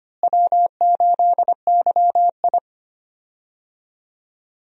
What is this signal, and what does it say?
Morse code, heard as a keyed tone, about 25 wpm, 700 Hz. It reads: W8YI
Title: USA Call Signs - 4 Characters - Repeated 3x - 25wpm